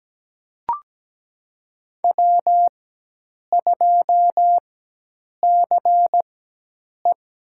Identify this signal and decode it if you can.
Morse code, heard as a keyed tone, about 17 wpm, spaced as for 8 wpm, 700 Hz. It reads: W2CE
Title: USA Call Signs - 4 Characters - 4x Character Spacing - 17wpm